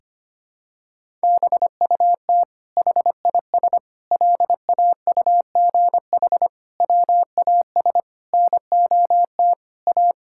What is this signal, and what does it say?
Morse code, heard as a keyed tone, about 25 wpm, 700 Hz. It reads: BUT HIS LAUGH WAS NOT A
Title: Sentences from Top 1000 Words - Review Part 5 - 25wpm